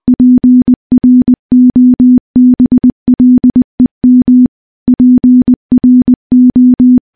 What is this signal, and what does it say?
Morse code, heard as a keyed tone, about 20 wpm, 250 Hz. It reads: PROBLEM PRO
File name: morse-training-20WPM